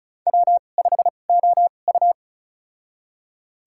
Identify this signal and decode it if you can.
Morse code, heard as a keyed tone, about 34 wpm, spaced as for 25 wpm, 700 Hz. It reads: W5OU
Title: USA Call Signs - 2x Character Spacing - 35wpm